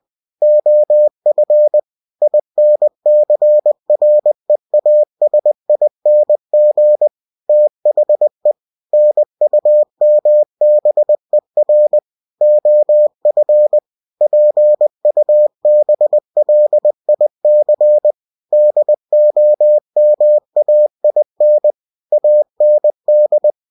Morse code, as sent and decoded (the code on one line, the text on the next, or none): --- ..-. / .. -. -.-. .-. .- ... .. -. --. / - .... / -. ..- -- -... .-. / --- ..-. / .--. ..- -... .-.. .. -.-. / -.. --- -- .- .. -. / .- -. -..
OF INCREASING THE NUMBER OF PUBLIC DOMAIN AND